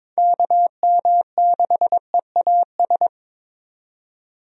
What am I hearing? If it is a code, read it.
Morse code, heard as a keyed tone, about 23 wpm, 700 Hz. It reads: KM6EAS
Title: USA Call Signs - 6 Characters - 22wpm